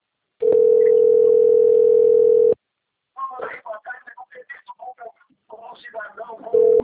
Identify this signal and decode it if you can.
Ringback tone (North America), 440 + 480 Hz.
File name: rg-1600-unknown- - - .20495